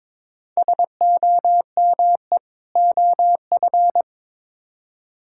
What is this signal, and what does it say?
Morse code, heard as a keyed tone, about 22 wpm, 700 Hz. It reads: SOME OF
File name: practice / 22wpm / Build A Phrase - 5-Word Phrases - 22wpm